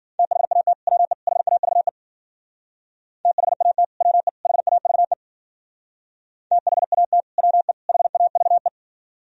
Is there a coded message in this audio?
Yes: Morse code from 0.2 to 8.7 s, heard as a keyed tone, about 52 wpm, 700 Hz. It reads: THAT WE HAVE THAT WE HAVE THAT WE HAVE